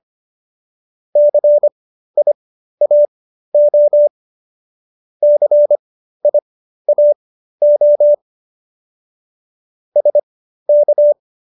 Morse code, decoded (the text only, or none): CIAO CIAO SK